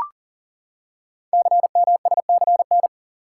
Morse code, heard as a keyed tone, about 44 wpm, 700 Hz. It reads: CMSCN
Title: Code Groups - 5 Letters - 40wpm